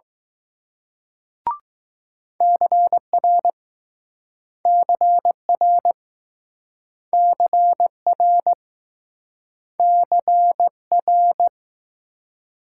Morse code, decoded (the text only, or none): CR CR CR CR